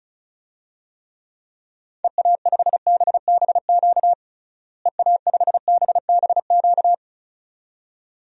Morse code, heard as a keyed tone, about 36 wpm, 700 Hz. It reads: EA5BBQ EA5BBQ
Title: DXCC-100 Call Signs - 6 Characters - Repeated 2x - 35wpm